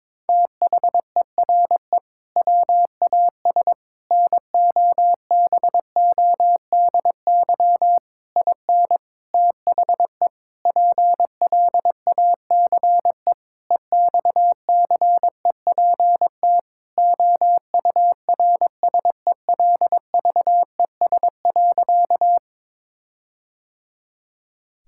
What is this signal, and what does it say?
Morse code, heard as a keyed tone, about 22 wpm, 700 Hz. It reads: THERE WAS NOBODY IN THE PLACE EXCEPT OURSELVES.
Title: Sentences from Top 1600 Words 22wpm